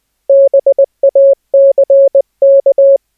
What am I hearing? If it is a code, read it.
Morse code, heard as a keyed tone, about 20 wpm, 550 Hz. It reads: BACK
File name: cw / cwfox / Op D 550 Hz